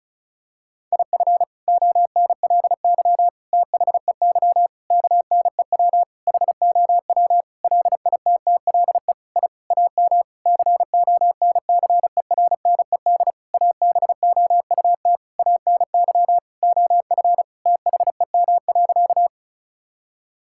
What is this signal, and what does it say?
Morse code, heard as a keyed tone, about 36 wpm, 700 Hz. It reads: IF ONLY THEY KNEW HOW LITTLE I AM CONCERNED ABOUT ANY OF THEM.
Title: Sentences from Top 1000 Words - Review Part 6 - 35wpm